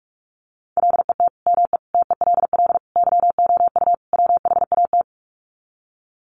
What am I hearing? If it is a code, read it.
Morse code, heard as a keyed tone, about 44 wpm, 700 Hz. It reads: LET ME TELL YOU WHAT
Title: Sets of 5 Words Encore 45wpm